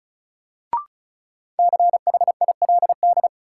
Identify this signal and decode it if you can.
Morse code, heard as a keyed tone, about 34 wpm, 700 Hz. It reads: CHILD
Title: Sets of 5 Words Encore - New Word Drill - 35wpm